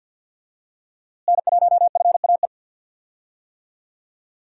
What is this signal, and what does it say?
Morse code, heard as a keyed tone, about 52 wpm, 700 Hz. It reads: N1FAE
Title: USA Call Signs - 5 Characters - Repeated 3x - 50wpm